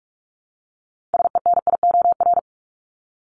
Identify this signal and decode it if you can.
Morse code, heard as a keyed tone, about 44 wpm, 700 Hz. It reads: SENIOR
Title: Top 1751-2000 Words 45wpm